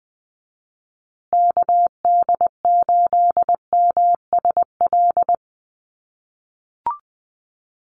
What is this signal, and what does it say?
Morse code, heard as a keyed tone, about 20 wpm, 700 Hz. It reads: KD8MSL
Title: USA Call Signs Repeated 3x 20wpm